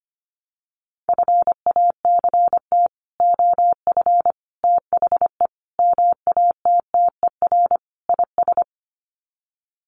Morse code, heard as a keyed tone, about 25 wpm, 700 Hz. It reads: FACT OF THE MATTER IS
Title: Sets of 5 Words 25wpm